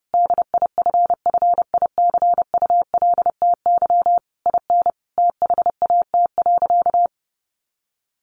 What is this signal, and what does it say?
Morse code, heard as a keyed tone, about 30 wpm, 700 Hz. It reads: DIFFICULTY IN THAT.
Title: Sentences from Top 1500 Words 30wpm